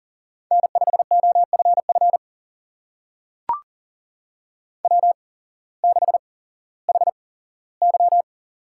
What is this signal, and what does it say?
Morse code, heard as a keyed tone, about 38 wpm, spaced as for 18 wpm, 700 Hz. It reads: N5OFF W6HY
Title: USA Call Signs - 8x Character Spacing - 40wpm